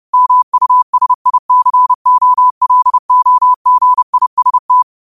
Morse code, heard as a keyed tone, about 30 wpm, 1000 Hz. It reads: MUSICOLOGIST
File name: qotdfiles / 30wpm